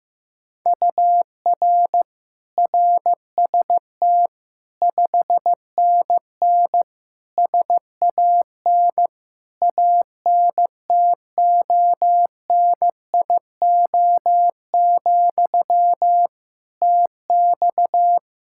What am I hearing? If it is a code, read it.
Morse code, heard as a keyed tone, about 15 wpm, 700 Hz. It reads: UR RST 5NN SAN ANTONIO, TX
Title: Practice QSO - Exchange 1 - Beginning - 15wpm